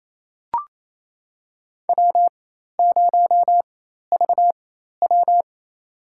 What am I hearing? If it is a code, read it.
Morse code, heard as a keyed tone, about 28 wpm, spaced as for 13 wpm, 700 Hz. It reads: W0VW
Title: USA Call Signs - 4x Character Spacing - 28wpm